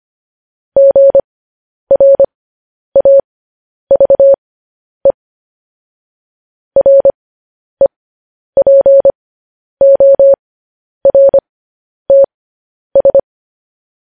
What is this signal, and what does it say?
Morse code, heard as a keyed tone, about 25 wpm, spaced as for 10 wpm, 550 Hz. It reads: GRAVE REPORTS
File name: podcast / News headlines-1x-25x10